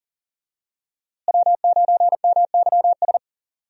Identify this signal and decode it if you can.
Morse code, heard as a keyed tone, about 40 wpm, 700 Hz. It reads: W9MYS
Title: USA Call Signs - 5 Characters - Repeated 2x - 40wpm